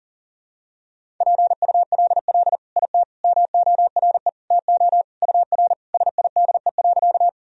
Morse code, heard as a keyed tone, about 40 wpm, 700 Hz. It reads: PULL IT MORE TO UR SIDE.
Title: Sentences from Top 400 Words and Abbreviations 40wpm